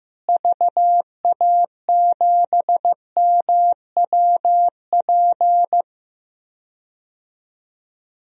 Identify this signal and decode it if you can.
Morse code, heard as a keyed tone, about 15 wpm, 700 Hz. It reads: VA7MWP
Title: DXCC-100 Call Signs - 6 Characters - Repeated 2x - 15wpm